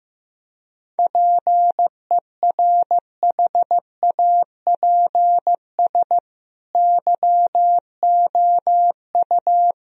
Morse code, heard as a keyed tone, about 16 wpm, 700 Hz. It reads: PERHAPS YOU
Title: Sentences from Top 1000 Words - Review Part 2 - 15wpm